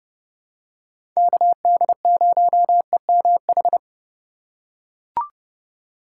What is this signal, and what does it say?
Morse code, heard as a keyed tone, about 30 wpm, 700 Hz. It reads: KD0EMH
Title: USA Call Signs - 6 Characters - 4x Character Spacing - 30wpm